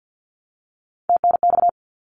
Morse code, heard as a keyed tone, about 50 wpm, 700 Hz. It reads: TNX
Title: Abbreviations 50wpm